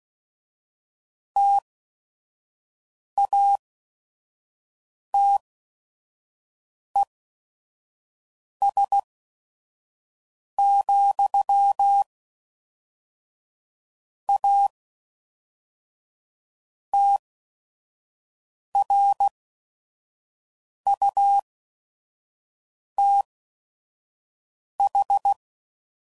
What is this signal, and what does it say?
Morse code, heard as a keyed tone, about 16 wpm, spaced as for 5 wpm, 800 Hz. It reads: TATES, A TRUTH
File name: CW Code Groups @5WPM / Technical Subjects / AND Gate